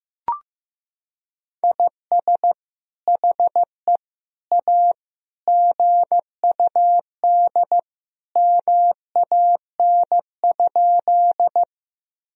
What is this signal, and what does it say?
Morse code, heard as a keyed tone, about 16 wpm, 700 Hz. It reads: IS HE A GUD MAN?
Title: Sentences from Top 200 Words and Abbreviations 15wpm